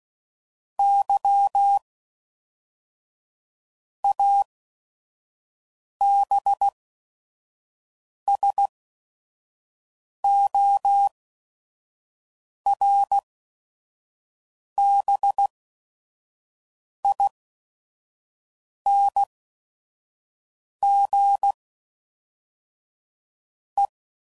Morse code, heard as a keyed tone, about 16 wpm, spaced as for 5 wpm, 800 Hz. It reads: Y ABSORBING E
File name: CW Code Groups @5WPM / Technical Subjects / XC